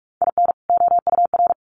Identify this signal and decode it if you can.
Morse code, heard as a keyed tone, about 44 wpm, 700 Hz. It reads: IN OUR
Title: Sets of 2 Words 45wpm